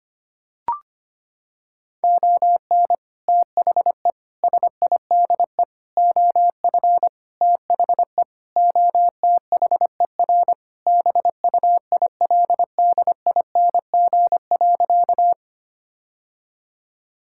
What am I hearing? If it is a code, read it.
Morse code, heard as a keyed tone, about 25 wpm, 700 Hz. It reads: ON THE SIDE OF THE OTHER BUILDING.
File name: practice / 25wpm / Sentences with 3-Word Phrases - 25wpm